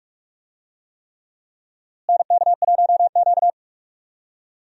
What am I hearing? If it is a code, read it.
Morse code, heard as a keyed tone, about 46 wpm, 700 Hz. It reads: NK1Q
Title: USA Call Signs - 4 Characters - Repeated 3x - 45wpm